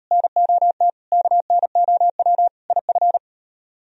Morse code, heard as a keyed tone, about 38 wpm, 700 Hz. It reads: NOT KNOW IF